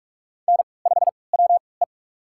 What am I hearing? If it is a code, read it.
Morse code, heard as a keyed tone, about 46 wpm, spaced as for 26 wpm, 700 Hz. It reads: N5WE
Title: USA Call Signs - 4 Characters - 3x Character Spacing - 45wpm